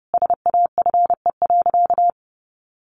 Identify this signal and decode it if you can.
Morse code, heard as a keyed tone, about 30 wpm, 700 Hz. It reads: SAFE.
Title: Sentences from Top 1300 Words 30wpm